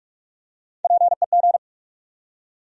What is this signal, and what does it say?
Morse code, heard as a keyed tone, about 48 wpm, 700 Hz. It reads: PEG
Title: USA Common Names - Max 3 Letters - 45wpm